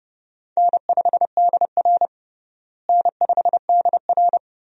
Morse code, heard as a keyed tone, about 32 wpm, 700 Hz. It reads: N5DR N5DR
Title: USA Call Signs - 4 Characters - Repeated 2x - 30wpm